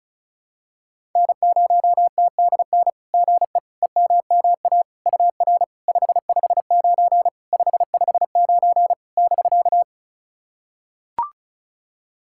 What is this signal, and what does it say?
Morse code, heard as a keyed tone, about 36 wpm, 690 Hz. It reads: N0TDN GE EMMA UR 559 559 <BK>
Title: SOTA QSO Practice 35wpm